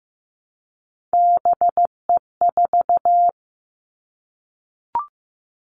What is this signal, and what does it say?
Morse code, heard as a keyed tone, about 15 wpm, 700 Hz. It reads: BE4
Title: Three Challenging Letter-Number Combinations 15wpm